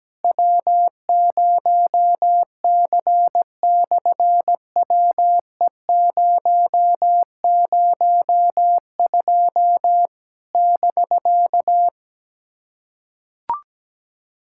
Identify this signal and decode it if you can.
Morse code, heard as a keyed tone, about 17 wpm, 700 Hz. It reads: W0C/WE002 <BK>
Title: SOTA QSO Practice 17wpm